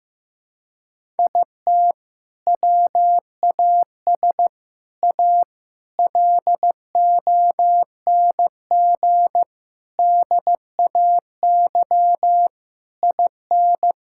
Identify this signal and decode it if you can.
Morse code, heard as a keyed tone, about 15 wpm, 700 Hz. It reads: IT WAS A LONG DAY IN
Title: Sentences from Top 1600 Words 15wpm